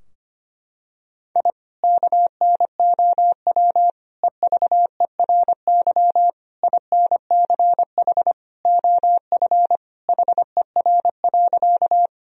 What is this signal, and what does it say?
Morse code, heard as a keyed tone, about 25 wpm, 700 Hz. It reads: I KNOW EVERY INCH OF HER.
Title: Sentences from Top 1000 Words - Review Part 3 - 25wpm